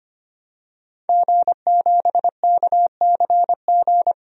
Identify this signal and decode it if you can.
Morse code, heard as a keyed tone, about 25 wpm, 700 Hz. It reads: G7KCG
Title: DXCC-100 Call Signs - 5 Characters - Repeated 3x - 25wpm